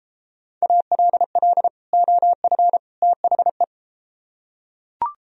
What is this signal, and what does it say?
Morse code, heard as a keyed tone, about 32 wpm, 700 Hz. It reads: ALL OF THE